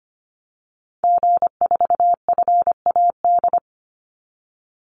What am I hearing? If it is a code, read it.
Morse code, heard as a keyed tone, about 25 wpm, 700 Hz. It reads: G4FAD
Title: Call Signs - 25wpm